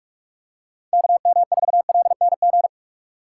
Morse code, heard as a keyed tone, about 42 wpm, 700 Hz. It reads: KM4LNG
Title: USA Call Signs - 6 Characters - 2x Character Spacing - 45wpm